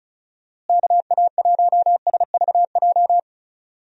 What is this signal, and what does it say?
Morse code, heard as a keyed tone, about 34 wpm, 700 Hz. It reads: KA1SVJ
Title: USA Call Signs - 6 Characters - Repeated 2x - 35wpm